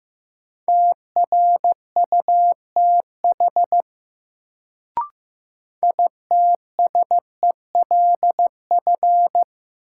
Morse code, heard as a keyed tone, about 15 wpm, 700 Hz. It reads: TRUTH ITSELF
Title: Sentences from Top 800 Words - New Word Drill - 15wpm